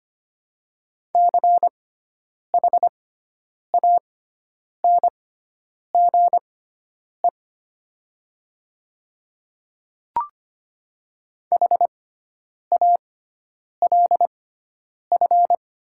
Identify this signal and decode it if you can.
Morse code, heard as a keyed tone, about 25 wpm, spaced as for 9 wpm, 700 Hz. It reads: CHANGE HALF
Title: 6x Character Spacing - Top 500 Words - 25wpm